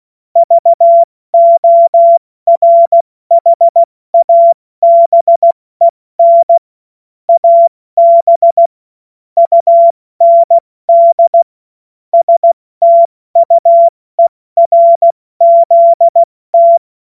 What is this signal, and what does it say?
Morse code, heard as a keyed tone, about 16 wpm, 680 Hz. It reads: VORHABEN AB UND STUERZT
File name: Witz-15-Juli-2025-70-F80